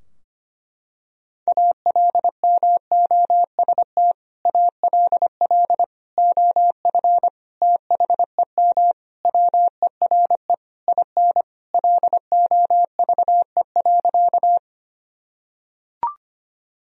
Morse code, heard as a keyed tone, about 25 wpm, 700 Hz. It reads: ALMOST ALL OF THEM WERE IN LOVE.
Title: Sentences from Top 1000 Words - Review Part 4 - 25wpm